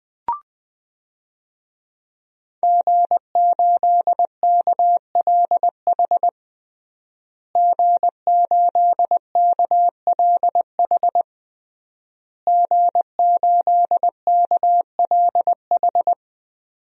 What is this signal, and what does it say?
Morse code, heard as a keyed tone, about 20 wpm, 700 Hz. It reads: G8KLH G8KLH G8KLH
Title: DXCC-100 Call Signs - 5 Characters - Repeated 3x - 20wpm